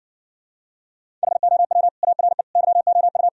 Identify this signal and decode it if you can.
Morse code, heard as a keyed tone, about 69 wpm, 700 Hz. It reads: HOW ARE YOU